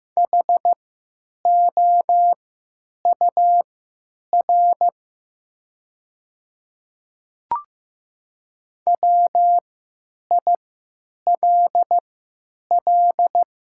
Morse code, heard as a keyed tone, about 15 wpm, spaced as for 9 wpm, 700 Hz. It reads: HOUR WILL
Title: 3x Character Spacing - Top 500 Words - 15wpm